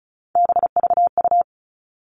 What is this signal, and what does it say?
Morse code, heard as a keyed tone, about 34 wpm, 700 Hz. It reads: BVU
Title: All Cut Numbers - 3-Digit 35wpm